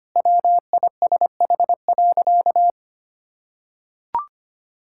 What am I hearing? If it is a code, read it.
Morse code, heard as a keyed tone, about 25 wpm, 700 Hz. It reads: WISH.
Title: Sentences from Top 1600 Words 25wpm